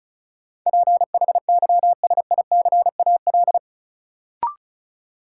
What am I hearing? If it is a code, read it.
Morse code, heard as a keyed tone, about 34 wpm, 700 Hz. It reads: PHYSICAL
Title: Sentences from Top 1000 Words - Review Part 4 - New Word Drill - 35wpm